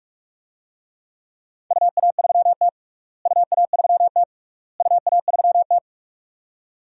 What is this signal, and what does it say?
Morse code, heard as a keyed tone, about 41 wpm, 700 Hz. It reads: UA3T UA3T UA3T
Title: DXCC-100 Call Signs - 4 Characters - Repeated 3x - 45wpm